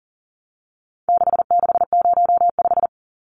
Morse code, heard as a keyed tone, about 40 wpm, 700 Hz. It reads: 6605
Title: Numbers - 4-Digit 40wpm